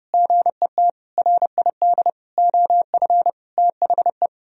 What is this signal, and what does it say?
Morse code, heard as a keyed tone, about 28 wpm, 700 Hz. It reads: GET RID OF THE